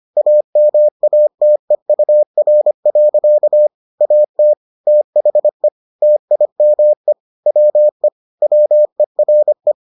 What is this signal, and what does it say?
Morse code, heard as a keyed tone, about 25 wpm, 600 Hz. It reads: AMATEUR. AT THE TIME WE WERE